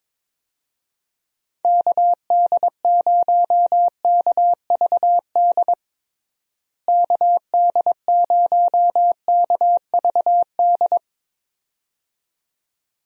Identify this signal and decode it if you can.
Morse code, heard as a keyed tone, about 22 wpm, 700 Hz. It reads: KD0KVD KD0KVD
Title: USA Call Signs - 6 Characters - Repeated 2x - 22wpm